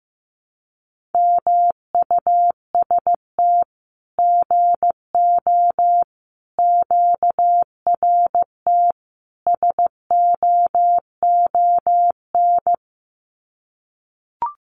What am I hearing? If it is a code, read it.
Morse code, heard as a keyed tone, about 15 wpm, 700 Hz. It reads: MUST GO QRT SOON
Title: Common QSO Phrases 15wpm